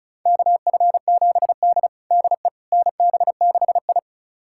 Morse code, heard as a keyed tone, about 36 wpm, 700 Hz. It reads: KF7D DE NB6I
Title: USA Callsign DE Callsign - 4 Characters - 35wpm